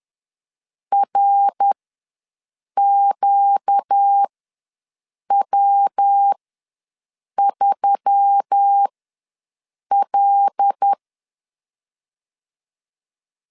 Morse code, heard as a keyed tone, about 10 wpm, spaced as for 6 wpm, 800 Hz. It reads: RQW3L
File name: LEZIONI / 5 LBVC / 5 L ws 10wpm 3636